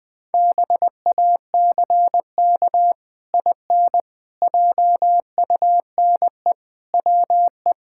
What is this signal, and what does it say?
Morse code, heard as a keyed tone, about 19 wpm, 700 Hz. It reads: BACK IN JUNE WE
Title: Pangram Challenge v2 - 20wpm